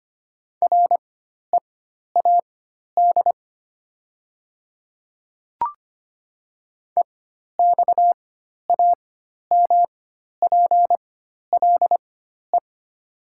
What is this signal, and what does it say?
Morse code, heard as a keyed tone, about 25 wpm, spaced as for 12 wpm, 690 Hz. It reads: READ EXAMPLE
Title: 4x Character Spacing - Top 300 Words - 25wpm